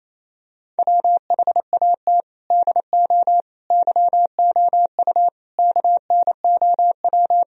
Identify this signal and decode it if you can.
Morse code, heard as a keyed tone, about 28 wpm, 700 Hz. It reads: WHAT DO YOU KNOW
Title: Sentences from Top 1000 Words - Review Part 4 - 28wpm